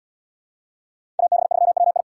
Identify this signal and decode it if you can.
Morse code, heard as a keyed tone, about 67 wpm, 700 Hz. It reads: AB3WI